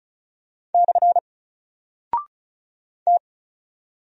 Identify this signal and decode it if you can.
Morse code, heard as a keyed tone, about 36 wpm, 700 Hz. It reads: / T
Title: Single Character (Letter:Number:Punctuation:Prosign) 35wpm